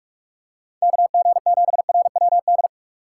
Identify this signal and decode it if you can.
Morse code, heard as a keyed tone, about 46 wpm, 700 Hz. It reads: KG7RWD
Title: USA Call Signs - 2x Character Spacing - 45wpm